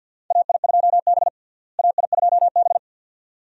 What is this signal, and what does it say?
Morse code, heard as a keyed tone, about 52 wpm, 700 Hz. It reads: AI2B AI2B
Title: USA Call Signs - 4 Characters - Repeated 2x - 50wpm